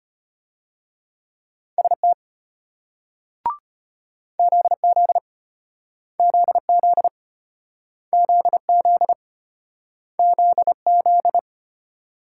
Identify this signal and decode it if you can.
Morse code, heard as a keyed tone, about 34 wpm, 700 Hz. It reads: ST ZZ ZZ ZZ ZZ